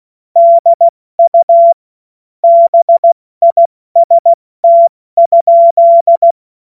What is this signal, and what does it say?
Morse code, heard as a keyed tone, about 16 wpm, 680 Hz. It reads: DU BIST?
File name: Morsekurs / Witze / Witz 24. August 2021, 70 F80